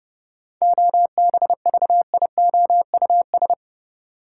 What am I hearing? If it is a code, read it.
Morse code, heard as a keyed tone, about 30 wpm, 700 Hz. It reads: OBVIOUS
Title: Sentences from Top 1600 Words - New Word Drill - 30wpm